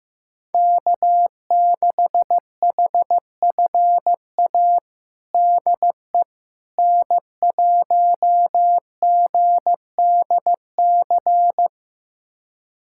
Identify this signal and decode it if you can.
Morse code, heard as a keyed tone, about 15 wpm, 700 Hz. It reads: K6HFA DE N1GDC
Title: USA Callsign DE Callsign - 5 Characters - 15wpm